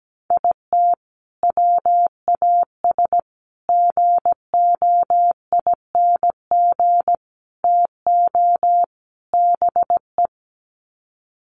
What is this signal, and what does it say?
Morse code, heard as a keyed tone, about 17 wpm, 700 Hz. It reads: IT WAS GOING TO BE
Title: Sets of 5 Words 17wpm